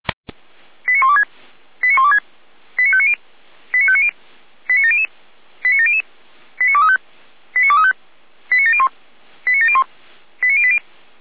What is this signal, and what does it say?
Five-tone selective call (ZVEI), heard as quick runs of five tones, tones of about 70 ms. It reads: 89126 89126 89590 89590 89700 89700 89235 89235 89701 89701 89080